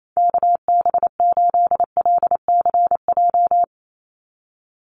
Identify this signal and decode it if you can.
Morse code, heard as a keyed tone, about 28 wpm, 700 Hz. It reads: KB8LCJ
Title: USA Call Signs 28wpm